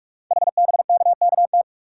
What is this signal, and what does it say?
Morse code, heard as a keyed tone, about 44 wpm, 690 Hz. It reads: HBKKT
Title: Code Groups - 5 Letters - 45wpm